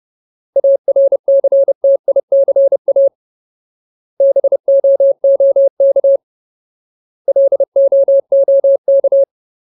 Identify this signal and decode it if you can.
Morse code, heard as a keyed tone, about 30 wpm, 550 Hz. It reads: ARCTICA BOOK LOOK
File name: Ultimate Word List / 30WPM Ultimate